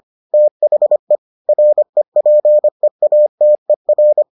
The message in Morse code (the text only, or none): THE REPEATER